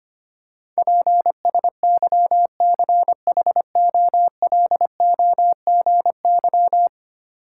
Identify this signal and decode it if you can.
Morse code, heard as a keyed tone, about 25 wpm, 700 Hz. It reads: PSYCHOLOGY
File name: Ten Letter Words - 25wpm